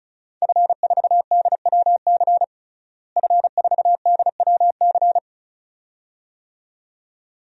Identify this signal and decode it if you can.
Morse code, heard as a keyed tone, about 36 wpm, 700 Hz. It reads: F4DWC F4DWC
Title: DXCC-100 Call Signs - 5 Characters - Repeated 2x - 35wpm